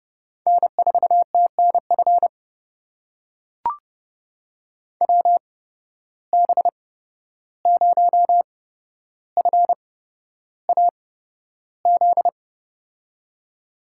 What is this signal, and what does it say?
Morse code, heard as a keyed tone, about 32 wpm, spaced as for 12 wpm, 700 Hz. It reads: N4TNF WB0FAZ
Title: USA Call Signs - 8x Character Spacing - 30wpm